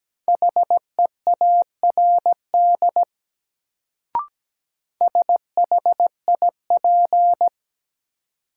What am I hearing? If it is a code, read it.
Morse code, heard as a keyed tone, about 17 wpm, 700 Hz. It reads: HEARD SHIP
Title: Sentences from Top 400 Words - New Word Drill - 17wpm